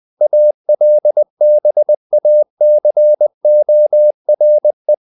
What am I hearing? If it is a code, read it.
Morse code, heard as a keyed tone, about 20 wpm, 600 Hz. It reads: ALBACORE
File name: the old man and the sea / the old man and the sea part 1 20wpm